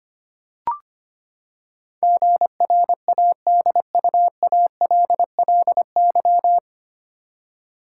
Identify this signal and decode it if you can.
Morse code, heard as a keyed tone, about 25 wpm, 700 Hz. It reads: GRADUALLY
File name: Nine Letter Words - 25wpm